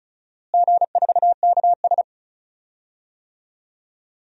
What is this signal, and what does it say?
Morse code, heard as a keyed tone, about 34 wpm, 700 Hz. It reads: G4KS
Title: DXCC-100 Call Signs - 4 Characters - Repeated 2x - 35wpm